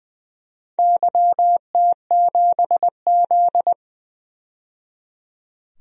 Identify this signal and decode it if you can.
Morse code, heard as a keyed tone, about 20 wpm, 700 Hz. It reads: YT7Z
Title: DXCC-100 Call Signs - 4 Characters - 20wpm